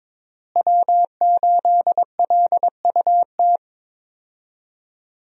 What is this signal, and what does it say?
Morse code, heard as a keyed tone, about 22 wpm, 700 Hz. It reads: W8LUT
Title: USA Call Signs - 5 Characters - Repeated 3x - 22wpm